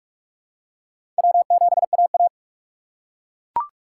Morse code, heard as a keyed tone, about 41 wpm, 700 Hz. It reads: W7AA
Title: USA Call Signs - 4 Characters - 3x Character Spacing - 45wpm